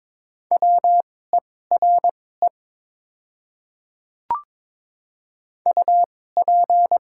Morse code, heard as a keyed tone, about 22 wpm, spaced as for 16 wpm, 700 Hz. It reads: WERE UP
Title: 2x Character Spacing - Top 100 Words - 22wpm